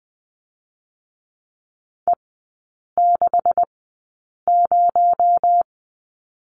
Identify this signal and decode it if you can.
Morse code, heard as a keyed tone, about 20 wpm, spaced as for 8 wpm, 700 Hz. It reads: E60
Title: 218 Lesson 35e - Number 0 - Three Characters 20wpm